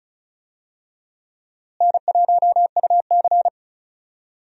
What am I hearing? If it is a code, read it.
Morse code, heard as a keyed tone, about 36 wpm, 700 Hz. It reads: N1UC
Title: USA Call Signs - 4 Characters - Repeated 2x - 35wpm